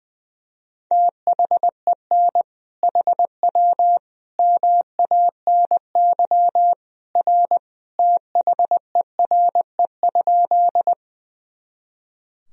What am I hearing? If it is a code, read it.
Morse code, heard as a keyed tone, about 20 wpm, 700 Hz. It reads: THEN HW MANY R THERE?
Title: Sentences from Top 100 Words and Abbreviations 20wpm